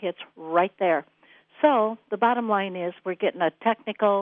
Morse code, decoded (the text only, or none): none